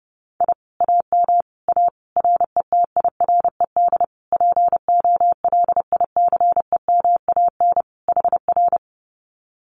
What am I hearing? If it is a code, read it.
Morse code, heard as a keyed tone, about 28 wpm, 700 Hz. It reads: I AM A RETIRED POLICEMAN HR
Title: Common QSO Phrases Encore 30wpm